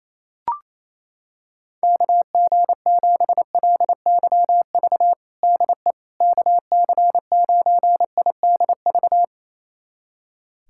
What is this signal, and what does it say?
Morse code, heard as a keyed tone, about 28 wpm, 700 Hz. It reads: KG7LYV DE KC9IDV
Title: USA Callsign DE Callsign - 6 Characters - 28wpm